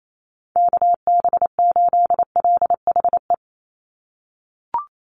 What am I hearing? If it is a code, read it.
Morse code, heard as a keyed tone, about 28 wpm, 700 Hz. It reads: KB8LHE
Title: USA Call Signs 28wpm